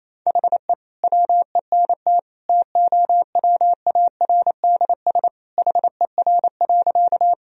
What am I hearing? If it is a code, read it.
Morse code, heard as a keyed tone, about 28 wpm, 690 Hz. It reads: HE WENT TOWARDS HER.
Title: Sentences from Top 1000 Words - Review Part 2 - 28wpm